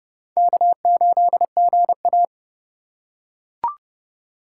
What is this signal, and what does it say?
Morse code, heard as a keyed tone, about 30 wpm, 700 Hz. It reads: K8GA